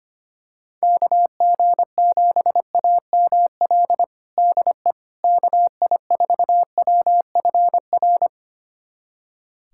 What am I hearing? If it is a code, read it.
Morse code, heard as a keyed tone, about 25 wpm, 700 Hz. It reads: KG7AML DE KI4WFR
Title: USA Callsign DE Callsign 25wpm